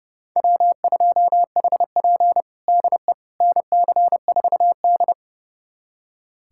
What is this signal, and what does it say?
Morse code, heard as a keyed tone, about 32 wpm, 700 Hz. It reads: W2HP DE NC4D
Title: USA Callsign DE Callsign - 4 Characters - 30wpm